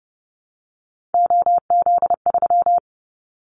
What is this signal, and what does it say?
Morse code, heard as a keyed tone, about 30 wpm, 700 Hz. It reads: OZ3
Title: Callsign Prefixes for Top 100 DXCC - 30wpm